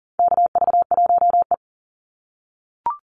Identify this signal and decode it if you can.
Morse code, heard as a keyed tone, about 40 wpm, 700 Hz. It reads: KV1E
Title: Call Signs - 40wpm